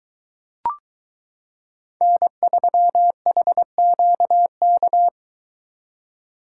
Morse code, heard as a keyed tone, about 23 wpm, 700 Hz. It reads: N3HQK